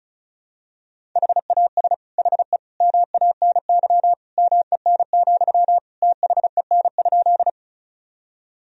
Morse code, heard as a keyed tone, about 36 wpm, 700 Hz. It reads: HAS HE MANY MEN, THEN?
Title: Sentences from Top 1000 Words - Review Part 1 - 35wpm